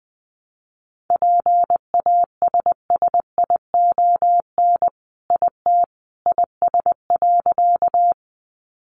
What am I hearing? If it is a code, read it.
Morse code, heard as a keyed tone, about 20 wpm, 700 Hz. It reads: PASSION IT IS.
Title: Sentences from Top 1300 Words 20wpm